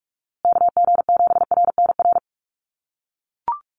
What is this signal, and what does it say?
Morse code, heard as a keyed tone, about 44 wpm, 700 Hz. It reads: KG7RNR
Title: USA Call Signs Repeated 2x 45wpm